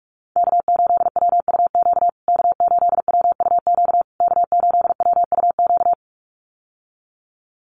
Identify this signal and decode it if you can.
Morse code, heard as a keyed tone, about 44 wpm, 700 Hz. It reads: K8WUQ K8WUQ K8WUQ
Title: USA Call Signs Repeated 3x 45wpm